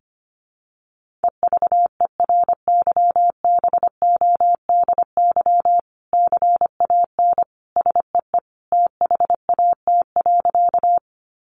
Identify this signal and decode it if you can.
Morse code, heard as a keyed tone, about 25 wpm, 700 Hz. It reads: EVERYBODY CAN SEE THAT.
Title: Sentences from Top 1400 Words 25wpm